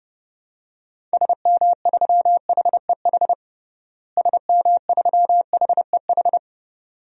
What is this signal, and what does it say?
Morse code, heard as a keyed tone, about 30 wpm, 700 Hz. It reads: SM3HEH SM3HEH
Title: DXCC-100 Call Signs - 6 Characters - Repeated 2x - 30wpm